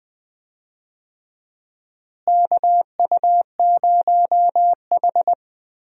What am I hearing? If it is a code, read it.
Morse code, heard as a keyed tone, about 21 wpm, 700 Hz. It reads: KU0H
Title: USA Call Signs - 4 Characters - Repeated 3x - 20wpm